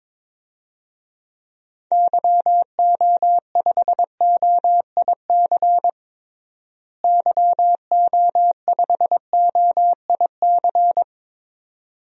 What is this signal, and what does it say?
Morse code, heard as a keyed tone, about 22 wpm, 700 Hz. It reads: YO5OIC YO5OIC
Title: DXCC-100 Call Signs - 6 Characters - Repeated 2x - 22wpm